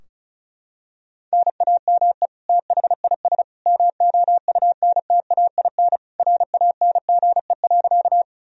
Morse code, heard as a keyed tone, about 34 wpm, 700 Hz. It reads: NAME THIS MOUNTAIN RANGE.
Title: Sentences from Top 1000 Words - Review Part 4 - 35wpm